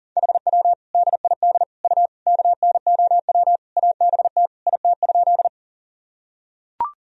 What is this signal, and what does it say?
Morse code, heard as a keyed tone, about 37 wpm, 700 Hz. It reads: HW DID U KNOW ABT IT?
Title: Sentences from Top 100 Words and Abbreviations 40wpm